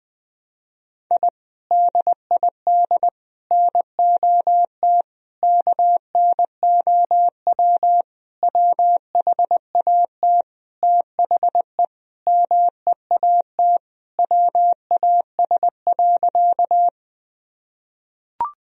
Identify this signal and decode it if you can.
Morse code, heard as a keyed tone, about 20 wpm, 700 Hz. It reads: I DID NOT KNOW WHAT THE MEAT WAS.
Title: Sentences from Top 1000 Words - Review Part 3 - 20wpm